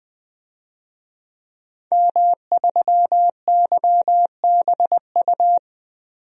Morse code, heard as a keyed tone, about 21 wpm, 700 Hz. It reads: M3YBU
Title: DXCC-100 Call Signs - 5 Characters - 20wpm